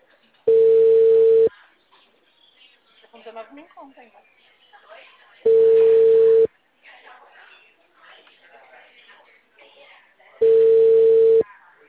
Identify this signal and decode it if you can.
Ringback tone (China, 450 Hz), with speech.